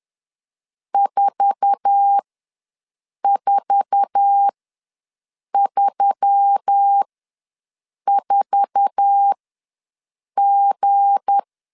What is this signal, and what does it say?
Morse code, heard as a keyed tone, about 10 wpm, spaced as for 6 wpm, 800 Hz. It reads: 4434G
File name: LEZIONI / 8 YJ45 / 8 YJ4 ws 10wpm 4020